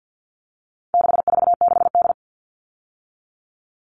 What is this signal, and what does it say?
Morse code, heard as a keyed tone, about 50 wpm, 700 Hz. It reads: 646D
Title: All Cut Numbers - 4-Digit 50wpm